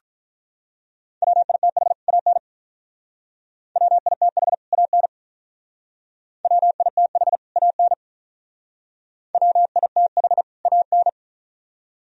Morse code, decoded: WITH AN WITH AN WITH AN WITH AN

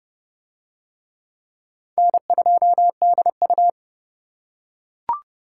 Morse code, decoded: N2DU

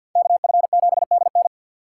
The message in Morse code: KF7DN

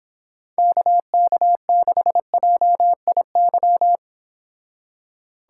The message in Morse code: KK6JIY

